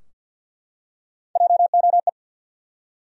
Morse code, heard as a keyed tone, about 53 wpm, 700 Hz. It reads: JOE
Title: USA Common Names - Max 3 Letters - 50wpm